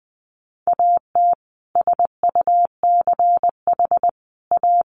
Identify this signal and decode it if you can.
Morse code, heard as a keyed tone, about 20 wpm, 700 Hz. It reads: AT SUCH A
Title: Sentences from Top 1000 Words 20wpm